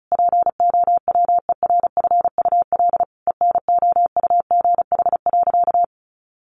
Morse code, heard as a keyed tone, about 34 wpm, 700 Hz. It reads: POWERFUL ENOUGH.
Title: Sentences from Top 1300 Words 35wpm